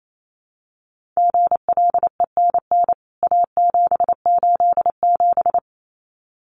Morse code, heard as a keyed tone, about 28 wpm, 700 Hz. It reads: GLENN A787
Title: CWT Exchange 28wpm